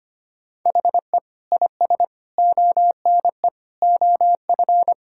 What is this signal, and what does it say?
Morse code, heard as a keyed tone, about 25 wpm, 700 Hz. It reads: HE IS ONE OF
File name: practice / 25wpm / Sentences with 4-Word Phrases - 25wpm